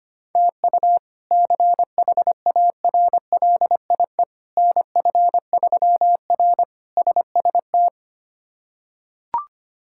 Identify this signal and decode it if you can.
Morse code, heard as a keyed tone, about 25 wpm, 700 Hz. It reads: TU CHARLIE NF3R SST
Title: K1USN SST QSO Practice - 25wpm